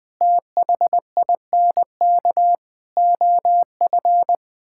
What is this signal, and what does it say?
Morse code, heard as a keyed tone, about 21 wpm, 700 Hz. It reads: THINK OF
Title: Sentences from Top 1600 Words 20wpm